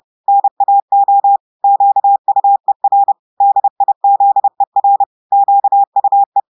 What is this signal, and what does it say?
Morse code, heard as a keyed tone, about 30 wpm, 800 Hz. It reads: NAO QUER DIZER QUE
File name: audios / portugais / portugais texte en clair 2 vitesse 30 mots minutes